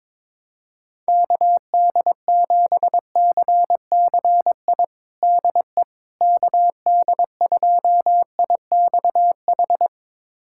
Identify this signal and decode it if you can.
Morse code, heard as a keyed tone, about 22 wpm, 700 Hz. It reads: KD7CCI DE KD2IXH
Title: USA Callsign DE Callsign - 6 Characters - 22wpm